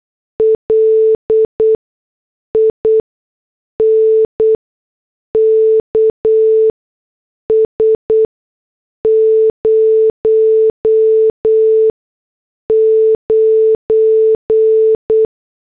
Morse code, heard as a keyed tone, about 8 wpm, spaced as for 6 wpm, 440 Hz. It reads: LINKS 09